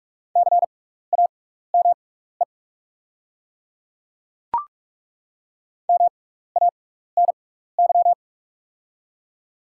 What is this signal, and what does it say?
Morse code, heard as a keyed tone, about 42 wpm, spaced as for 15 wpm, 700 Hz. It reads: CAME MANY
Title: 6x Character Spacing - Top 200 Words - 45wpm